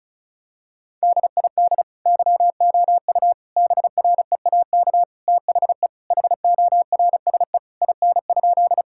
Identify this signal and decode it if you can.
Morse code, heard as a keyed tone, about 36 wpm, 700 Hz. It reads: DID YOU BREAK THE HORSE IN?
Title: Sentences from Top 1000 Words - Review Part 6 - 35wpm